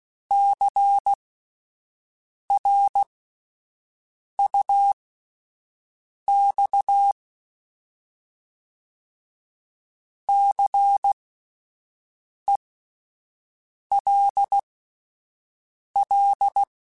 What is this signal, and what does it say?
Morse code, heard as a keyed tone, about 16 wpm, spaced as for 5 wpm, 800 Hz. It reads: CRUX CELL
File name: Spaced Alphas / Spaced Alphas mp3s / Cs sp @10